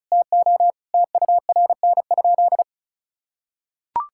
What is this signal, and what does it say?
Morse code, heard as a keyed tone, about 34 wpm, 700 Hz. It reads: TO TURN?
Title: Sentences from Top 1600 Words 35wpm